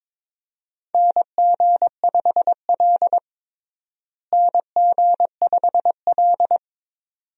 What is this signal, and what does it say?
Morse code, heard as a keyed tone, about 22 wpm, 700 Hz. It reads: NG5L NG5L
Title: USA Call Signs - 4 Characters - Repeated 2x - 22wpm